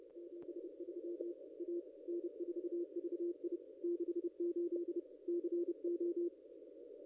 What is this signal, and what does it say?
Morse code, heard as a keyed tone, about 30 wpm, 350 Hz. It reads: 73 A N4VI 68 CO
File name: contests / ss-cw-k5tr-op-n5rz -0200